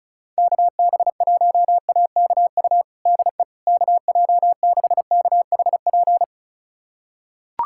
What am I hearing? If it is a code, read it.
Morse code, heard as a keyed tone, about 36 wpm, 700 Hz. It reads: KB1AKU DE KJ6KHP
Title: USA Callsign DE Callsign - 6 Characters - 35wpm